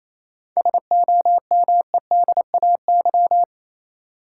Morse code, heard as a keyed tone, about 28 wpm, 700 Hz. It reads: SOMEDAY
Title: Sentences from Top 1600 Words - New Word Drill Alt - 28wpm